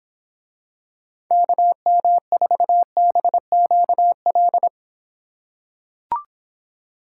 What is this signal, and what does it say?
Morse code, heard as a keyed tone, about 26 wpm, 700 Hz. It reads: KM4BQL